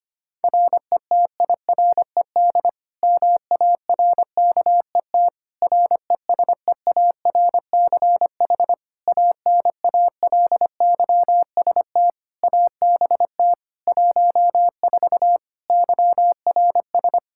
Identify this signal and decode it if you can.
Morse code, heard as a keyed tone, about 25 wpm, 700 Hz. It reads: RETIRED MARKET RESEARCH ANALYST ABT 14 YRS
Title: Practice QSO - Exchange 3 - More Info - 25wpm